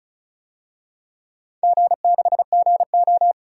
Morse code, heard as a keyed tone, about 36 wpm, 700 Hz. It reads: G6GO
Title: DXCC-100 Call Signs - 4 Characters - 35wpm